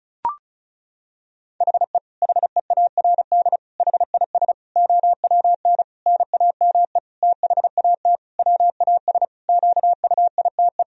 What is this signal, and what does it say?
Morse code, heard as a keyed tone, about 36 wpm, 700 Hz. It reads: HE HEARD HIS OWN NAME THAT WAS QUITE
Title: Sentences from Top 1000 Words - Review Part 2 - 35wpm